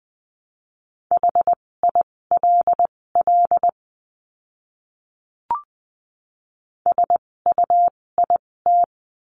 Morse code, heard as a keyed tone, about 20 wpm, spaced as for 16 wpm, 700 Hz. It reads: HILL SUIT
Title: 256 Lesson 42a - Words 12-20wpm